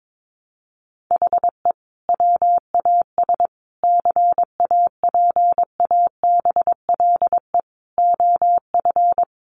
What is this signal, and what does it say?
Morse code, heard as a keyed tone, about 22 wpm, 700 Hz. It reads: HE WAS CAPABLE OF
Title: Sentences from Top 1200 Words 22wpm